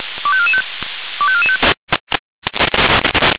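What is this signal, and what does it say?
Five-tone selective call (ZVEI), heard as quick runs of five tones, tones of about 70 ms. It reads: 25445 25445
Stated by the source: ZVEI five-tone selective call received on SDR.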